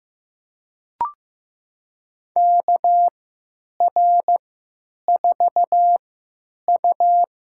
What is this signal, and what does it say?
Morse code, heard as a keyed tone, about 15 wpm, spaced as for 9 wpm, 700 Hz. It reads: KR4U
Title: USA Call Signs - 4 Characters - 3x Character Spacing - 15wpm